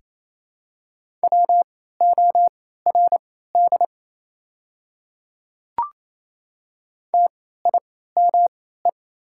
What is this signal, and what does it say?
Morse code, heard as a keyed tone, about 27 wpm, spaced as for 16 wpm, 700 Hz. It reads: WORD TIME
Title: 3x Character Spacing - Top 100 Words - 28wpm